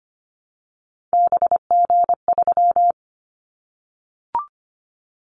Morse code, heard as a keyed tone, about 25 wpm, 700 Hz. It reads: BG3
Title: Callsign Prefixes for Top 100 DXCC - 25wpm